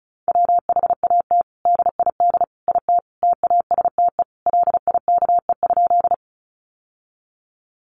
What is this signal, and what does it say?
Morse code, heard as a keyed tone, about 34 wpm, 700 Hz. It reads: WHAT DID IT TASTE LIKE?
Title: Sentences from Top 1100 Words 35wpm